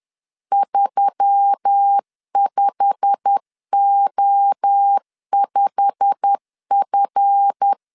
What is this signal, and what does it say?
Morse code, heard as a keyed tone, about 10 wpm, 800 Hz. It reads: 35O5F
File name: LEZIONI / 8 YJ45 / 8 YJ45 10wpm 2549